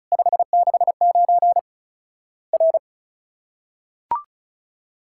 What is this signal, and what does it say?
Morse code, heard as a keyed tone, about 34 wpm, 700 Hz. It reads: 569 R
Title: Worked All Europe DX Contest - QTC Exchange - 35wpm